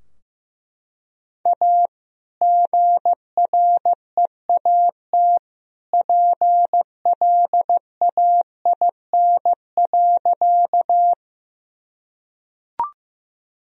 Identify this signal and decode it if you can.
Morse code, heard as a keyed tone, about 15 wpm, 700 Hz. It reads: A GREAT PLAIN.
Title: Sentences from Top 1000 Words - Review Part 2 - 15wpm